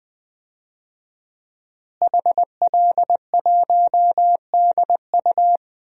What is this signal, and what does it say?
Morse code, heard as a keyed tone, about 20 wpm, 700 Hz. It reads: HL1DU
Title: DXCC-100 Call Signs - 5 Characters - Repeated 2x - 20wpm